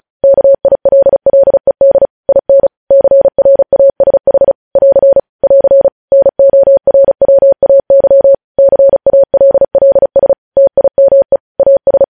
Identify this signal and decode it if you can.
Morse code, heard as a keyed tone, about 36 wpm, 550 Hz. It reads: KILLED IN CRASH <AR> <AR> NORWAY CALLS TIME AS
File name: podcast / News headlines-1x-35x35